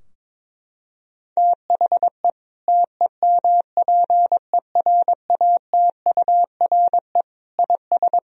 Morse code, heard as a keyed tone, about 22 wpm, 700 Hz. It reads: THE TEMPERATURE IS
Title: Sentences from Top 1000 Words - Review Part 4 - 22wpm